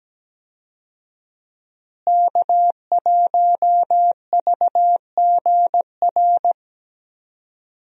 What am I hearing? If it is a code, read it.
Morse code, heard as a keyed tone, about 17 wpm, 700 Hz. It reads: K1VGR
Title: USA Call Signs - 5 Characters - Repeated 2x - 17wpm